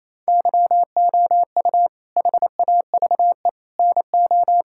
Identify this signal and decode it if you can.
Morse code, heard as a keyed tone, about 28 wpm, 700 Hz. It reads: YOU HAVE NO
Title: Sentences from Top 1000 Words - Review Part 5 - 28wpm